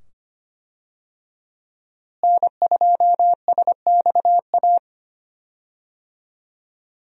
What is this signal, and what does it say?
Morse code, heard as a keyed tone, about 25 wpm, 700 Hz. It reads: N2SXA
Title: USA Call Signs - 5 Characters - Repeated 2x - 25wpm